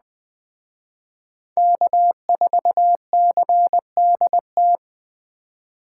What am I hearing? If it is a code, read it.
Morse code, heard as a keyed tone, about 20 wpm, 700 Hz. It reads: K4CDT